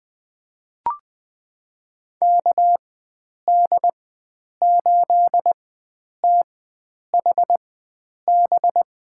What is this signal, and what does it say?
Morse code, heard as a keyed tone, about 20 wpm, spaced as for 9 wpm, 700 Hz. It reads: KD8THB
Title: USA Call Signs - 6 Characters - 4x Character Spacing - 20wpm